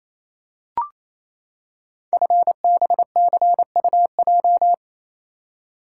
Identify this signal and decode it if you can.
Morse code, heard as a keyed tone, about 28 wpm, 700 Hz. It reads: FBCUJ
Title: Code Groups - 5 Letters - 28wpm